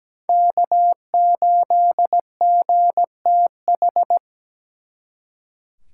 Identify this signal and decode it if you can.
Morse code, heard as a keyed tone, about 20 wpm, 700 Hz. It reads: K8GTH
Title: Speed Racing - USA 1x3 Call Signs 17wpm